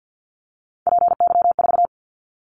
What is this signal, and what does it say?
Morse code, heard as a keyed tone, about 50 wpm, 700 Hz. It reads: PY4
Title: Callsign Prefixes for Top 100 DXCC - 50wpm